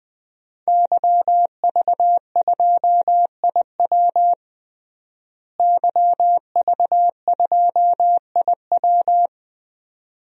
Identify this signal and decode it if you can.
Morse code, heard as a keyed tone, about 21 wpm, 700 Hz. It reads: YV2IW YV2IW
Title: DXCC-100 Call Signs - 5 Characters - Repeated 2x - 20wpm